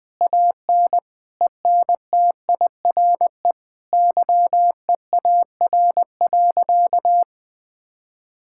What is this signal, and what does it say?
Morse code, heard as a keyed tone, about 21 wpm, 700 Hz. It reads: AN ENTIRE YEAR.
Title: Sentences from Top 1000 Words - Review Part 5 - 20wpm